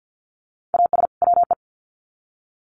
Morse code, heard as a keyed tone, about 50 wpm, 700 Hz. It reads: AS WE
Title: Sets of 2 Words 50wpm